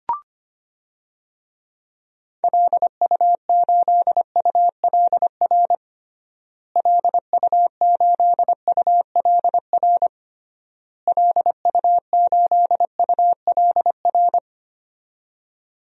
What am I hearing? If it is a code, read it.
Morse code, heard as a keyed tone, about 25 wpm, 700 Hz. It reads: LU8ULR LU8ULR LU8ULR
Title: DXCC-100 Call Signs - 6 Characters - Repeated 3x - 25wpm